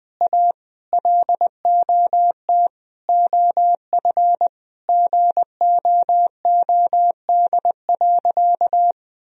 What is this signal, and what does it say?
Morse code, heard as a keyed tone, about 20 wpm, 700 Hz. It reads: A LOT OF GOOD.
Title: Sentences from Top 1000 Words - Review Part 3 - 20wpm